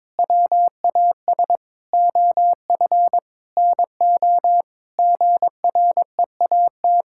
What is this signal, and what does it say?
Morse code, heard as a keyed tone, about 22 wpm, 700 Hz. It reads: WAS OF NO GREAT
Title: Sentences from Top 1600 Words 22wpm